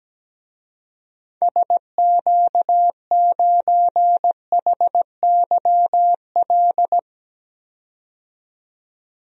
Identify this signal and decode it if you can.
Morse code, heard as a keyed tone, about 17 wpm, 700 Hz. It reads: SQ9HYL
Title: DXCC-100 Call Signs - 6 Characters - 17wpm